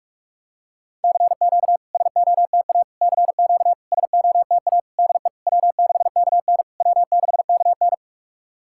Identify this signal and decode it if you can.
Morse code, heard as a keyed tone, about 48 wpm, 700 Hz. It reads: CQ SOTA CQ SOTA DE W6KN W6KN
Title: SOTA QSO Practice 45wpm